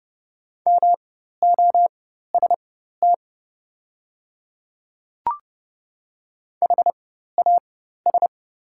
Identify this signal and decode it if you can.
Morse code, heard as a keyed tone, about 30 wpm, spaced as for 14 wpm, 700 Hz. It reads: MOST HAS
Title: 4x Character Spacing - Top 100 Words - 30wpm